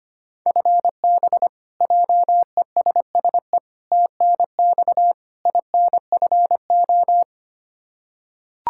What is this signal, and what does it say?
Morse code, heard as a keyed tone, about 25 wpm, 700 Hz. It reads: FB JESSE TNX INFO
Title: K1USN SST QSO Practice 25wpm